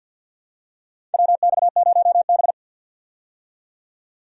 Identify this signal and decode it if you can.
Morse code, heard as a keyed tone, about 52 wpm, 700 Hz. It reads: WX0B
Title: USA Call Signs - 4 Characters - 50wpm